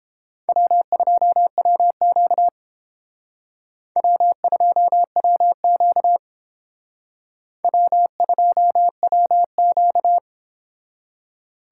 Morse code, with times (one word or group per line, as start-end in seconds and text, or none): 0.5-2.5 W2WQ
4.0-6.2 W2WQ
7.6-10.2 W2WQ